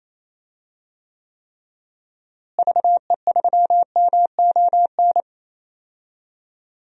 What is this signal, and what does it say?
Morse code, heard as a keyed tone, about 28 wpm, 700 Hz. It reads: VE3MON
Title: DXCC-100 Call Signs - 6 Characters - Repeated 3x - 28wpm